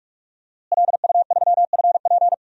Morse code, heard as a keyed tone, about 47 wpm, 700 Hz. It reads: LU3FP
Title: DXCC-100 Call Signs - 5 Characters - Repeated 2x - 45wpm